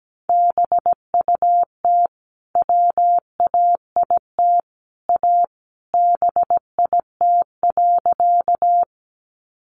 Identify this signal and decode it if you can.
Morse code, heard as a keyed tone, about 17 wpm, 700 Hz. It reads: BUT WAIT A BIT.
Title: Sentences from Top 900 Words 17wpm